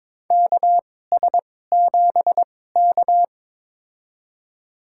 Morse code, heard as a keyed tone, about 21 wpm, spaced as for 16 wpm, 700 Hz. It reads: KS7K
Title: USA Call Signs - 4 Characters - 2x Character Spacing - 22wpm